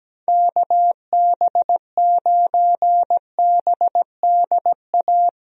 Morse code, heard as a keyed tone, about 17 wpm, 700 Hz. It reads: KB9BDA
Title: USA Call Signs - 6 Characters - Repeated 2x - 17wpm